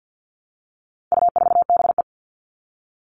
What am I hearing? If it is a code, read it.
Morse code, heard as a keyed tone, about 50 wpm, 700 Hz. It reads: U4BE
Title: All Cut Numbers - 4-Digit 50wpm